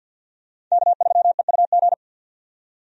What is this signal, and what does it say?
Morse code, heard as a keyed tone, about 51 wpm, 700 Hz. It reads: K3EUG
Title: USA Call Signs - 5 Characters - 50wpm